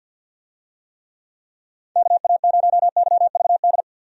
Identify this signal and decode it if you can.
Morse code, heard as a keyed tone, about 52 wpm, 700 Hz. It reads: KA0YVD
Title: USA Call Signs - 6 Characters - 50wpm